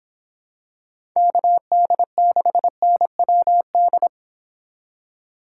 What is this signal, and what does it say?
Morse code, heard as a keyed tone, about 26 wpm, 700 Hz. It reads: KD6NWD